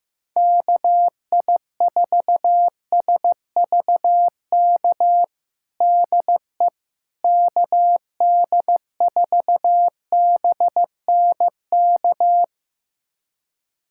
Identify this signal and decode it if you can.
Morse code, heard as a keyed tone, about 16 wpm, 700 Hz. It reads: KI4SVK DE KD4BNK
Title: USA Callsign DE Callsign - 6 Characters - 15wpm